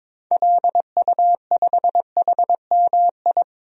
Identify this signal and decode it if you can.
Morse code, heard as a keyed tone, about 22 wpm, 700 Hz. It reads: LU5HMI
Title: DXCC-100 Call Signs - 6 Characters - 22wpm